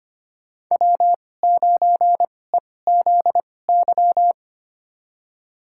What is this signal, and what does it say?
Morse code, heard as a keyed tone, about 25 wpm, spaced as for 18 wpm, 700 Hz. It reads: W9EZY
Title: USA Call Signs - 5 Characters - 2x Character Spacing - 25wpm